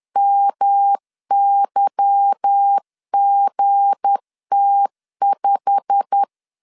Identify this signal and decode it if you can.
Morse code, heard as a keyed tone, about 10 wpm, 800 Hz. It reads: MYGT5
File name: CW / DET / 1 9 1500C S A 10wpm 4511